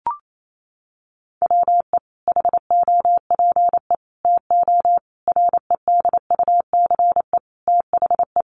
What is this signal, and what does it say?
Morse code, heard as a keyed tone, about 28 wpm, 700 Hz. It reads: WE HOPE TO REDUCE THE
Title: Sentences from Top 1100 Words 28wpm